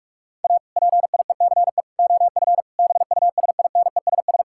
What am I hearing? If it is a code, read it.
Morse code, heard as a keyed tone, about 42 wpm, 700 Hz. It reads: A PIECE OF BUSINESS
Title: Sentences from Top 1000 Words - Review Part 5 - 45wpm